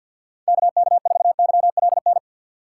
Morse code, heard as a keyed tone, about 49 wpm, 700 Hz. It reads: KK4YLN
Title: USA Call Signs - 6 Characters - 4x Character Spacing - 50wpm